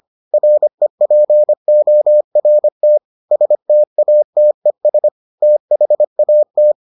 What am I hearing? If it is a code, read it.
Morse code, heard as a keyed tone, about 25 wpm, 600 Hz. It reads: REPORT STATES THAT